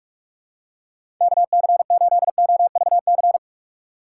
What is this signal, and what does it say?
Morse code, heard as a keyed tone, about 42 wpm, 700 Hz. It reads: KC8OVC
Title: USA Call Signs - 6 Characters - 2x Character Spacing - 45wpm